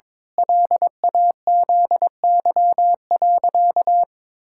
Morse code, heard as a keyed tone, about 22 wpm, 700 Hz. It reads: LAZY.
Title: Pangram Challenge v2 - 22wpm